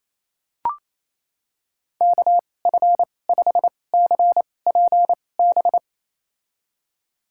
Morse code, heard as a keyed tone, about 28 wpm, spaced as for 20 wpm, 690 Hz. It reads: KF5CPB
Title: USA Call Signs - 6 Characters - 2x Character Spacing - 28wpm